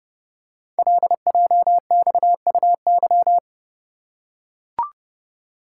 Morse code, heard as a keyed tone, about 30 wpm, 700 Hz. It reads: LJXUY
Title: Code Groups - 5 Letter:Numbers - 30wpm